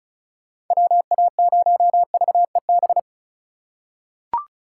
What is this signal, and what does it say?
Morse code, heard as a keyed tone, about 34 wpm, 700 Hz. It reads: WA0VEB
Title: USA Call Signs - 6 Characters - 3x Character Spacing - 35wpm